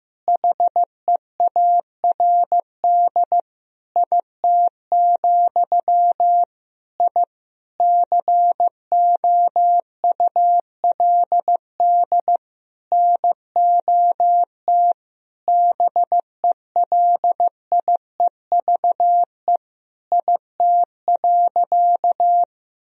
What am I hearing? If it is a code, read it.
Morse code, heard as a keyed tone, about 15 wpm, 700 Hz. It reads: HEARD IT, I COULD NOT BELIEVE IT.
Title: Sentences from Top 1000 Words - Review Part 3 - 15wpm